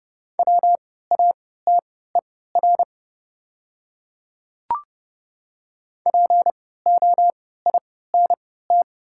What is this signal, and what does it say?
Morse code, heard as a keyed tone, about 31 wpm, spaced as for 17 wpm, 690 Hz. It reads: WATER POINT
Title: 3x Character Spacing - Top 200 Words - 30wpm